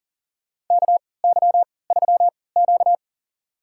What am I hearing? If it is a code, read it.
Morse code, heard as a keyed tone, about 41 wpm, spaced as for 23 wpm, 700 Hz. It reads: KY3Q
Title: USA Call Signs - 4 Characters - 3x Character Spacing - 40wpm